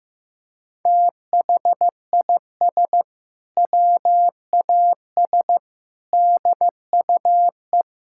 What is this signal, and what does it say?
Morse code, heard as a keyed tone, about 15 wpm, 700 Hz. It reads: THIS WAS DUE
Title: Sentences from Top 1600 Words 15wpm